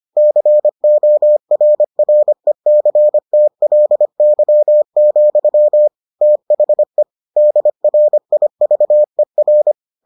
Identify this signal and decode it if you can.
Morse code, heard as a keyed tone, about 25 wpm, 600 Hz. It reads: CORRECTLY, THE DRIVER